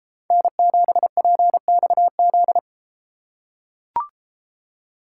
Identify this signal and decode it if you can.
Morse code, heard as a keyed tone, about 32 wpm, 700 Hz. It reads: N7PXZ